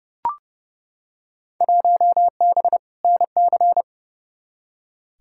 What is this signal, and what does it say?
Morse code, heard as a keyed tone, about 30 wpm, 700 Hz. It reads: 1B NC
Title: Field Day Exchanges - 30wpm